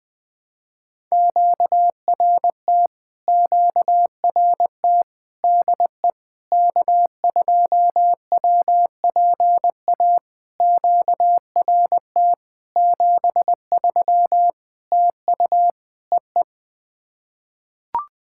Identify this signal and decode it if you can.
Morse code, heard as a keyed tone, about 20 wpm, 700 Hz. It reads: QRT QRT DE K2WPA QRT 73 TU EE
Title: POTA QSO Practice - 20wpm